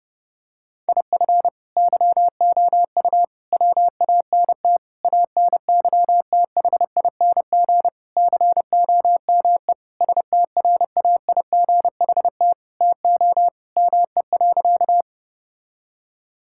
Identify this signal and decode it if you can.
Morse code, heard as a keyed tone, about 30 wpm, 700 Hz. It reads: IF YOU WANT ANYTHING COME STRAIGHT TO ME.
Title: Sentences from Top 1000 Words - Review Part 6 - 30wpm